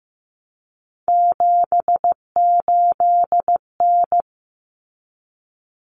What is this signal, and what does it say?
Morse code, heard as a keyed tone, about 15 wpm, 700 Hz. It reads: 78N
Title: Common Cut Numbers - 3-Digit 15wpm